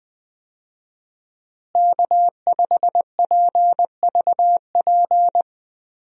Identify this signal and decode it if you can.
Morse code, heard as a keyed tone, about 20 wpm, 700 Hz. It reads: K5PVP
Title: USA Call Signs - 5 Characters - Repeated 3x - 20wpm